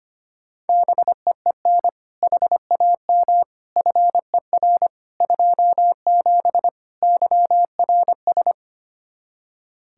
Morse code, heard as a keyed tone, about 25 wpm, 700 Hz. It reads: BEEN HAM FER 27 YRS
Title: Practice QSO - Exchange 3 - More Info - 25wpm